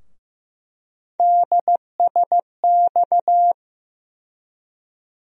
Morse code, heard as a keyed tone, about 15 wpm, 700 Hz. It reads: DSX
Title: USA Call Sign Postfixes - 3 Letters 15wpm